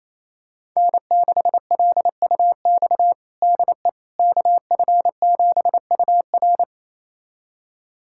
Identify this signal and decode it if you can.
Morse code, heard as a keyed tone, about 28 wpm, 700 Hz. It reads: N6LUX DE KF7UR
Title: USA Callsign DE Callsign - 5 Characters - 28wpm